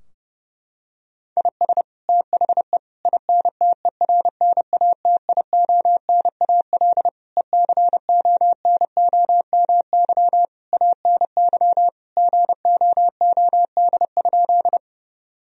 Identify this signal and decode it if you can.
Morse code, heard as a keyed tone, about 31 wpm, 700 Hz. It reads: IS THE INTERNATIONAL ECONOMY ANY GOOD?
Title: Sentences from Top 1000 Words - Review Part 4 - 30wpm